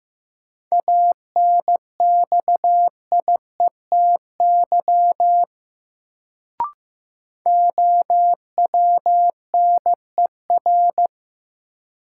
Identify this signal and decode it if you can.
Morse code, heard as a keyed tone, about 15 wpm, 700 Hz. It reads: ANXIETY OWNER
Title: Sentences from Top 1600 Words - New Word Drill - 15wpm